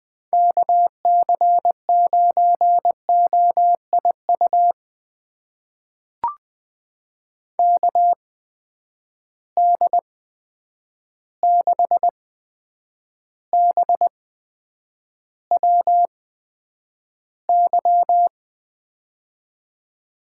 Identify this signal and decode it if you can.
Morse code, heard as a keyed tone, about 20 wpm, spaced as for 9 wpm, 700 Hz. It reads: KC9OIU KD6BWY
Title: USA Call Signs - 6 Characters - 8x Character Spacing - 20wpm